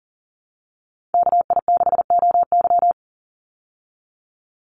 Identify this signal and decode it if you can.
Morse code, heard as a keyed tone, about 40 wpm, 700 Hz. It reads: KI6OY
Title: Call Signs Repeated 2x - 40wpm